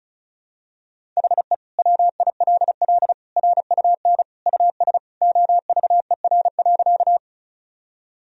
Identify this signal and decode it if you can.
Morse code, heard as a keyed tone, about 34 wpm, 700 Hz. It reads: HE WILL RUN US OVER.
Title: Sentences from Top 1000 Words - Review Part 1 - 35wpm